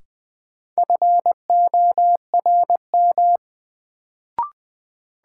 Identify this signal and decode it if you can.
Morse code, heard as a keyed tone, about 20 wpm, 700 Hz. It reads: FORM
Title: Sets of 4 Words - New Word Drill - 20wpm